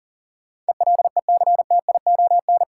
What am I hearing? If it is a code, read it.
Morse code, heard as a keyed tone, about 40 wpm, 700 Hz. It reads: ELECTION
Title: Sentences from Top 1500 Words - New Word Drill Alt - 40wpm